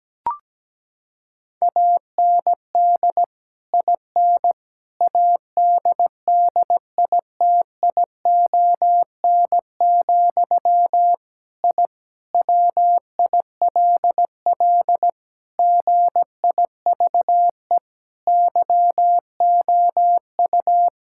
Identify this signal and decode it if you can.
Morse code, heard as a keyed tone, about 17 wpm, 700 Hz. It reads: AND IN ADDITION, I WILL GIVE YOU
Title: Sentences from Top 1000 Words - Review Part 6 - 17wpm